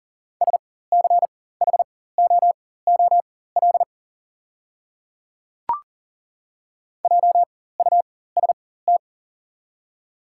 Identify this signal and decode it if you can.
Morse code, heard as a keyed tone, about 40 wpm, spaced as for 19 wpm, 700 Hz. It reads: SCHOOL JUST
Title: 4x Character Spacing - Top 300 Words - 40wpm